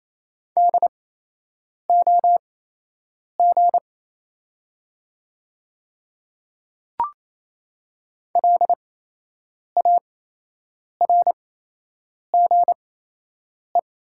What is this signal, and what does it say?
Morse code, heard as a keyed tone, about 28 wpm, spaced as for 8 wpm, 700 Hz. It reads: DOG LARGE
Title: 8x Character Spacing - Top 500 Words - 28wpm